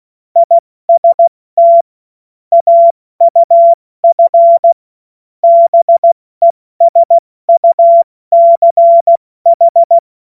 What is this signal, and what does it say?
Morse code, heard as a keyed tone, about 16 wpm, 680 Hz. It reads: IST AUF BESUCH
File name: Morsekurs / Witze / Witz 30. Dezember 2025 70 F80